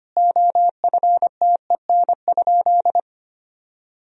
Morse code, heard as a keyed tone, about 25 wpm, 700 Hz. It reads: OFTEN?
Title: Sentences from Top 1000 Words - Review Part 4 - 25wpm